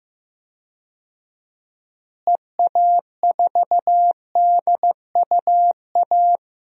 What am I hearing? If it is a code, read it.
Morse code, heard as a keyed tone, about 15 wpm, 700 Hz. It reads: EA4DUA
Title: DXCC-100 Call Signs - 6 Characters - Repeated 3x - 15wpm